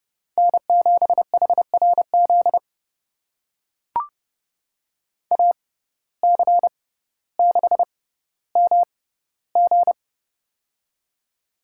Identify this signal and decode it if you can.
Morse code, heard as a keyed tone, about 31 wpm, spaced as for 15 wpm, 700 Hz. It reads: N7HRZ AC6MG
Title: USA Call Signs - 5 Characters - 6x Character Spacing - 30wpm